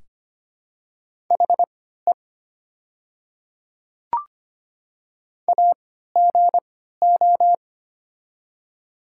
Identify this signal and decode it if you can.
Morse code, heard as a keyed tone, about 25 wpm, spaced as for 14 wpm, 700 Hz. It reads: HE AGO
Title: 3x Character Spacing - Top 500 Words - 25wpm